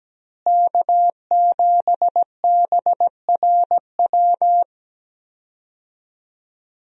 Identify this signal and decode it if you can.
Morse code, heard as a keyed tone, about 17 wpm, 700 Hz. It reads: K7BRW
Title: USA Call Signs - 5 Characters - Repeated 3x - 17wpm